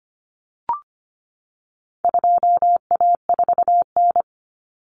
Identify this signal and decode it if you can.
Morse code, heard as a keyed tone, about 25 wpm, 700 Hz. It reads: 2A4N
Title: Common Cut Numbers - 4-Digit 25wpm